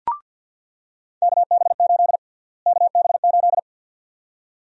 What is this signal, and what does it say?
Morse code, heard as a keyed tone, about 48 wpm, 700 Hz. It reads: KB8 KB8
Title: USA Call Sign Prefixes - Repeated 2x - 50wpm